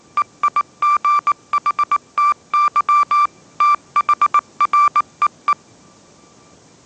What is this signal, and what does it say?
Morse code, heard as a keyed tone, about 19 wpm, 1170 Hz. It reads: EIGHTY THREE
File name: mrx083 20wpm